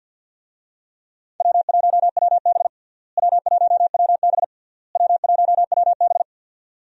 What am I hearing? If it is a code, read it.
Morse code, heard as a keyed tone, about 52 wpm, 700 Hz. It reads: W1WB W1WB W1WB
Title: USA Call Signs - 4 Characters - Repeated 3x - 50wpm